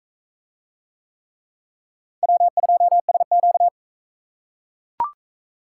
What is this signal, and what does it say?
Morse code, heard as a keyed tone, about 38 wpm, 700 Hz. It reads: W2SQ